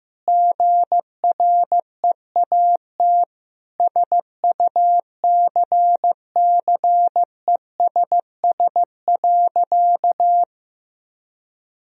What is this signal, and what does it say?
Morse code, heard as a keyed tone, about 15 wpm, 700 Hz. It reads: GREAT SUCCESS.
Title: Sentences from Top 1000 Words - Review Part 4 - 15wpm